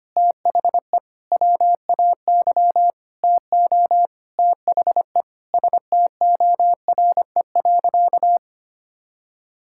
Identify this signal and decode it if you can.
Morse code, heard as a keyed tone, about 25 wpm, 700 Hz. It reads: THE WAY TO THE STORE.
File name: practice / 25wpm / Sentences with 9-Word Phrases - 25wpm